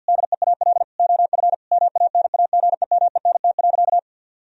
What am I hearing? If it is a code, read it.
Morse code, heard as a keyed tone, about 50 wpm, 700 Hz. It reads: DEAL OF MANAGEMENT.
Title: Sentences from Top 1000 Words - Review Part 4 - 50wpm